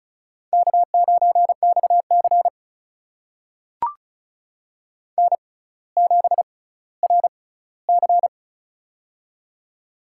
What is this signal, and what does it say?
Morse code, heard as a keyed tone, about 34 wpm, spaced as for 18 wpm, 700 Hz. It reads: K9XC N7RC
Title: USA Call Signs - 4 Characters - 6x Character Spacing - 35wpm